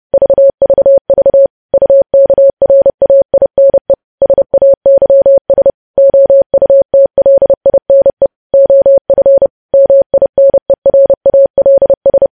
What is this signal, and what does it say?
Morse code, heard as a keyed tone, about 30 wpm, 550 Hz. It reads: VVV UKRAINE SAYS OUTLINE OF MINERALS
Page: News Headlines in Morse Code at 30 WPM